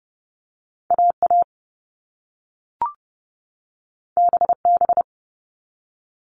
Morse code, heard as a keyed tone, about 30 wpm, 700 Hz. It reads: AA BB
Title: All Cut Numbers - 2-Digit 30wpm